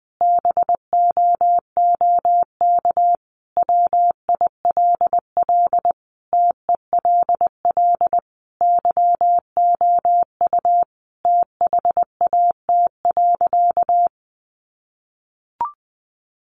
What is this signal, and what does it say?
Morse code, heard as a keyed tone, about 20 wpm, 700 Hz. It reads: BOOK WILL TELL YOU THAT.
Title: Sentences from Top 300 Words 20wpm